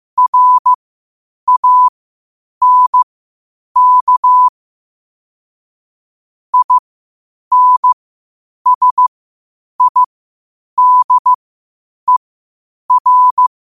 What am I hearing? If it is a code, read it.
Morse code, heard as a keyed tone, about 15 wpm, spaced as for 8 wpm, 1000 Hz. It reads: RANK INSIDER
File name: qotdfiles / 8wpm